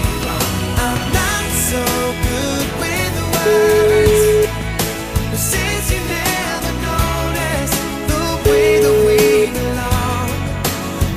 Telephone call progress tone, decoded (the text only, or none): ringback tone